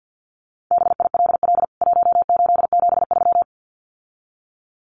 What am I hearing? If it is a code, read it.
Morse code, heard as a keyed tone, about 50 wpm, 700 Hz. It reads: BILL 1873
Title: CWT Exchange 50wpm